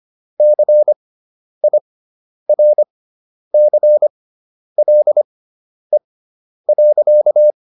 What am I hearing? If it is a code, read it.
Morse code, heard as a keyed tone, about 25 wpm, spaced as for 10 wpm, 600 Hz. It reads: CIRCLE.